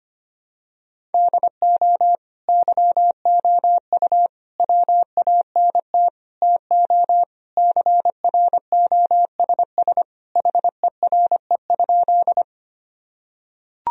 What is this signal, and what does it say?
Morse code, heard as a keyed tone, about 25 wpm, 700 Hz. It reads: DO YOU WANT TO CROSS HERE?
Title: Sentences from Top 1000 Words - Review Part 1 - 25wpm